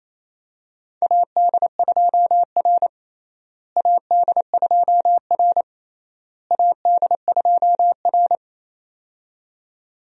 Morse code, heard as a keyed tone, about 28 wpm, 700 Hz. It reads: AD2R AD2R AD2R
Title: USA Call Signs - 4 Characters - Repeated 3x - 28wpm